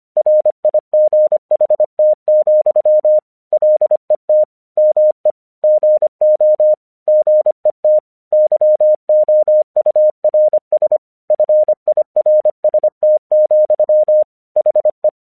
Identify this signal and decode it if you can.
Morse code, heard as a keyed tone, about 26 wpm, 610 Hz. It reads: RIGHT, LET ME GO GET YOURS FIRST, HE
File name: 1 English 25wpm / AQC20